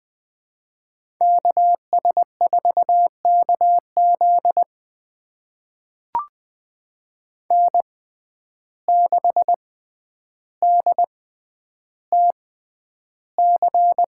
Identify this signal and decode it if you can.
Morse code, heard as a keyed tone, about 20 wpm, spaced as for 10 wpm, 700 Hz. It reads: KS4KZ N6DTC
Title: USA Call Signs - 6x Character Spacing - 20wpm